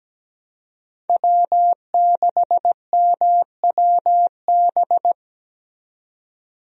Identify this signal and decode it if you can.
Morse code, heard as a keyed tone, about 17 wpm, 700 Hz. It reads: W6MWB
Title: USA Call Signs - 5 Characters - 17wpm